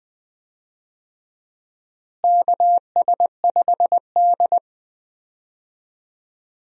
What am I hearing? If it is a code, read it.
Morse code, heard as a keyed tone, about 20 wpm, 700 Hz. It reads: KS5D
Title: USA Call Signs - 4 Characters - Repeated 3x - 20wpm